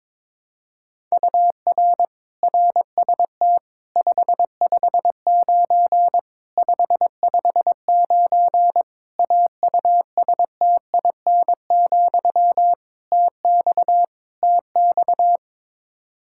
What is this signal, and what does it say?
Morse code, heard as a keyed tone, about 22 wpm, 700 Hz. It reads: UR RST 559 559 AUSTIN, TX TX
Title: K1USN SST QSO Practice 22wpm